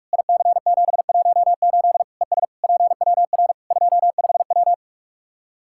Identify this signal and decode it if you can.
Morse code, heard as a keyed tone, about 46 wpm, 700 Hz. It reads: IC718 ES PWR 25W
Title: Practice QSO - Exchange 2 - Info - 45wpm